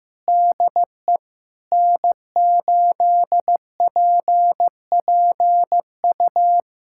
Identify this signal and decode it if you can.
Morse code, heard as a keyed tone, about 16 wpm, 700 Hz. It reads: DE N8PPU
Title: USA Callsign DE Callsign - 5 Characters - 15wpm